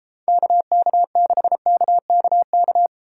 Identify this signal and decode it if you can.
Morse code, heard as a keyed tone, about 31 wpm, 700 Hz. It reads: KK6KKK